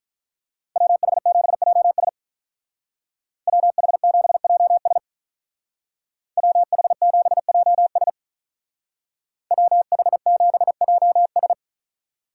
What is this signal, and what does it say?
Morse code, heard as a keyed tone, about 43 wpm, 700 Hz. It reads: WH7JS WH7JS WH7JS WH7JS
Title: Speed Racing - USA 2x2 Call Signs 35wpm